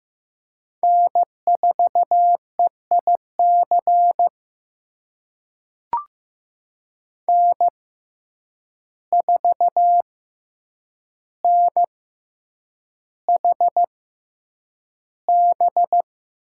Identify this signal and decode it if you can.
Morse code, heard as a keyed tone, about 16 wpm, spaced as for 8 wpm, 700 Hz. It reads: N4EIC N4NHB
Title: USA Call Signs - 5 Characters - 6x Character Spacing - 15wpm